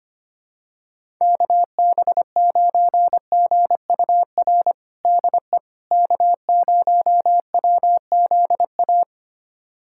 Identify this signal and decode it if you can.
Morse code, heard as a keyed tone, about 25 wpm, 700 Hz. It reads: KB9GUR DE K0WZA
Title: USA Callsign DE Callsign 25wpm